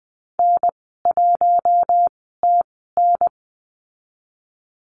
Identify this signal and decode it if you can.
Morse code, heard as a keyed tone, about 20 wpm, spaced as for 14 wpm, 700 Hz. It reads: N1TN
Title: 042 Lesson 8g - Number 1 - Call Signs 20wpm